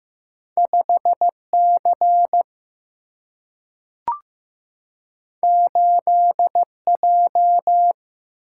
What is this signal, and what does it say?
Morse code, heard as a keyed tone, about 15 wpm, 700 Hz. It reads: 5 C 8 J
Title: Single-Letter Single-Number 15wpm